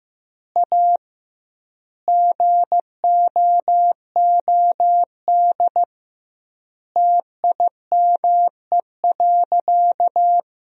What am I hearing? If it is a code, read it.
Morse code, heard as a keyed tone, about 15 wpm, 700 Hz. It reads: A GOOD TIME.
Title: 2x Word-Spacing - Sentences from Top 200 Words - 15wpm